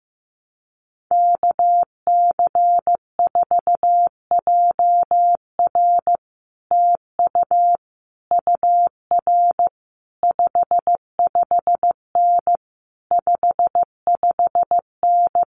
Morse code, heard as a keyed tone, about 15 wpm, 700 Hz. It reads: KC4JR TU UR 55N 55N
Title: Common QSO Phrases Encore 15wpm